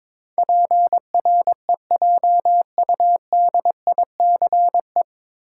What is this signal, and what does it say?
Morse code, heard as a keyed tone, about 22 wpm, 700 Hz. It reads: PREJUDICE
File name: 9 Letter Words - First 3 Letters Given - 22wpm